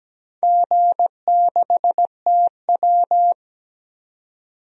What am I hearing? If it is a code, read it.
Morse code, heard as a keyed tone, about 17 wpm, 700 Hz. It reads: G6TW
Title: DXCC-100 Call Signs - 4 Characters - 17wpm